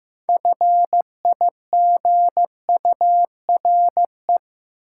Morse code, heard as a keyed tone, about 15 wpm, 700 Hz. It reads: FIGURE
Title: Sentences from Top 1400 Words - New Word Drill Alt - 15wpm